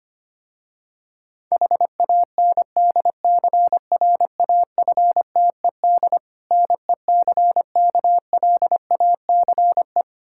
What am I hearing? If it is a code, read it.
Morse code, heard as a keyed tone, about 25 wpm, 700 Hz. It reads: HANDCRAFTED NECKLACE
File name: practice / 25wpm / Difficult Words with Hints - 25wpm